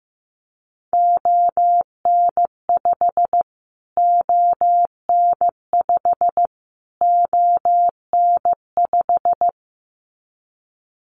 Morse code, heard as a keyed tone, about 15 wpm, 700 Hz. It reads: ON5 ON5 ON5
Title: Callsign Prefixes for Top 25 DXCC - Repeated 3x - 15wpm